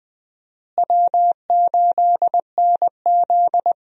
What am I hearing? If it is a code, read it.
Morse code, heard as a keyed tone, about 20 wpm, 700 Hz. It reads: W8NZ
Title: USA Call Signs - 4 Characters - Repeated 3x - 20wpm